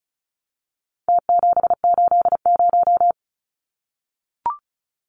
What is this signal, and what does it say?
Morse code, heard as a keyed tone, about 34 wpm, 700 Hz. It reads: T780
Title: Numbers - Contest Style - 4-Digit 35wpm